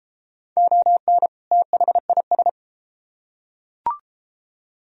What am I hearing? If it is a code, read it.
Morse code, heard as a keyed tone, about 32 wpm, 700 Hz. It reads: ON THIS